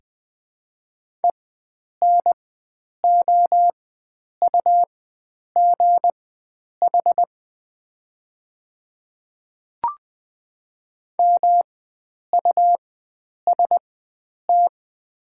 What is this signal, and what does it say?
Morse code, heard as a keyed tone, about 20 wpm, spaced as for 9 wpm, 700 Hz. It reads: ENOUGH MUST
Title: 4x Character Spacing - Top 500 Words - 20wpm